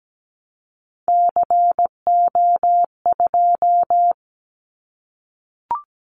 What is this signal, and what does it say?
Morse code, heard as a keyed tone, about 17 wpm, 700 Hz. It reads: CO2
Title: Callsign Prefixes for Top 100 DXCC - 17wpm